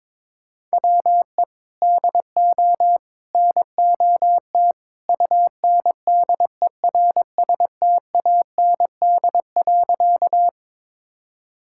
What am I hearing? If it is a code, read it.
Morse code, heard as a keyed tone, about 22 wpm, 700 Hz. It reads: WE DO NOT UNDERSTAND.
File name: practice / 22wpm / Sentences with 3-Word Phrases - 22wpm